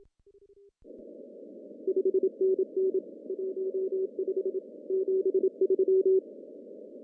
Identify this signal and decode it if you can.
Morse code, heard as a keyed tone, about 24 wpm, 340 Hz. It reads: 5NN 15 73
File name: CW / cqww-cw-k5tr